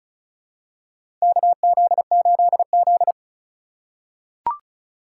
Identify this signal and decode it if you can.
Morse code, heard as a keyed tone, about 36 wpm, 700 Hz. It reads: KZ8Z
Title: USA Call Signs - 4 Characters - 2x Character Spacing - 35wpm